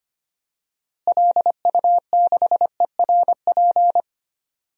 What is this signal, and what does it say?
Morse code, heard as a keyed tone, about 25 wpm, 700 Hz. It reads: LU6ERP
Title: DXCC-100 Call Signs - 6 Characters - Repeated 2x - 25wpm